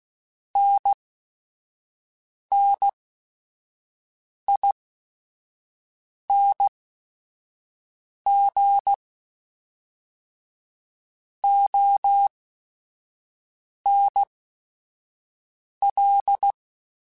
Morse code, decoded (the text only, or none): NNING ONL